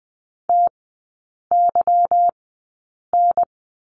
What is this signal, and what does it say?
Morse code, heard as a keyed tone, about 20 wpm, spaced as for 8 wpm, 700 Hz. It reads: TYN
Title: 184 Lesson 30e - Letter b - Three Characters 20wpm